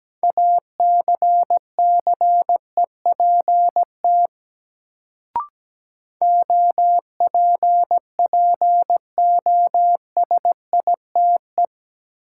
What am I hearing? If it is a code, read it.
Morse code, heard as a keyed tone, about 17 wpm, 700 Hz. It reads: ACCEPT OPPOSITE
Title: Sentences from Top 1300 Words - New Word Drill Alt - 17wpm